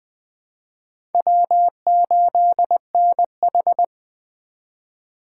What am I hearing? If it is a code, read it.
Morse code, heard as a keyed tone, about 20 wpm, 700 Hz. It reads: W8NH
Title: USA Call Signs - 3x Character Spacing - 20wpm